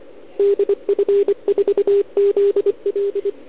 Morse code, heard as a keyed tone, about 26 wpm, 410 Hz. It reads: DF4ZL
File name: ARCHIV / OKDX2020 / OL4Wa